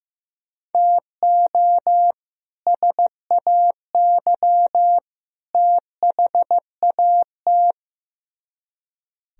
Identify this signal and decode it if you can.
Morse code, heard as a keyed tone, about 15 wpm, 700 Hz. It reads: TO SAY THAT
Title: Speed-Racing - Sets of 3 Words 15wpm